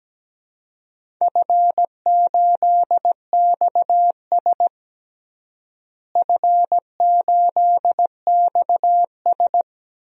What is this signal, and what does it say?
Morse code, heard as a keyed tone, about 17 wpm, 700 Hz. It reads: F8XS F8XS
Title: DXCC-100 Call Signs - 4 Characters - Repeated 3x - 17wpm